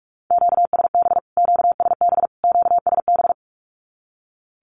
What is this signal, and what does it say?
Morse code, heard as a keyed tone, about 44 wpm, 700 Hz. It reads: QSB QSB QSB
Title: Q-Codes Repeated 3x 45wpm